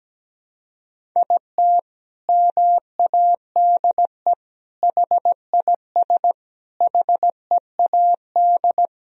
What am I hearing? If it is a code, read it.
Morse code, heard as a keyed tone, about 17 wpm, 700 Hz. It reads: IT MADE HIS HEAD
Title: Sentences from Top 1000 Words - Review Part 1 - 17wpm